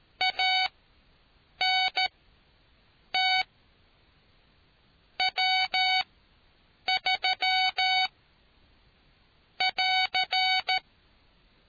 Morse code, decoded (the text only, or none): ANT W3 <AR>